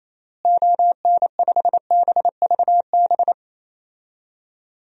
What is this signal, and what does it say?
Morse code, heard as a keyed tone, about 28 wpm, 700 Hz. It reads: ON5BVB
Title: DXCC-100 Call Signs - 6 Characters - Repeated 3x - 28wpm